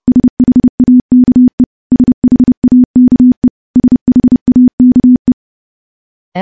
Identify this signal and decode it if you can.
Morse code, heard as a keyed tone, about 30 wpm, 250 Hz. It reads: SHAKE SHAKE SHAKE
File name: morse-training-30WPM